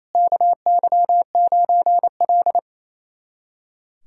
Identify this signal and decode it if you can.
Morse code, heard as a keyed tone, about 28 wpm, 700 Hz. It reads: KY9L
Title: USA Callsign DE Callsign 28wpm